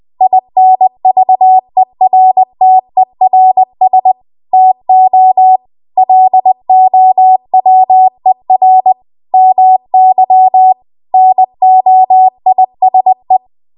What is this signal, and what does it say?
Morse code, heard as a keyed tone, about 20 wpm, 750 Hz. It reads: INVERTERS TO LOWER MY NOISE
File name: Morse / 20WPM